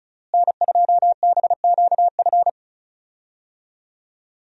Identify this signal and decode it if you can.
Morse code, heard as a keyed tone, about 34 wpm, 700 Hz. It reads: N2BQF
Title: USA Call Signs - 5 Characters - Repeated 3x - 35wpm